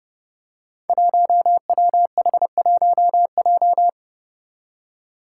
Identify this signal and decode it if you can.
Morse code, heard as a keyed tone, about 30 wpm, 700 Hz. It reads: 1WH1J
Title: Five Challenging Letter-Number Combinations 30wpm